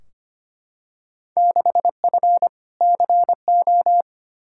Morse code, heard as a keyed tone, about 25 wpm, 700 Hz. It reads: 6F CO
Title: Field Day Exchanges Encore - 25wpm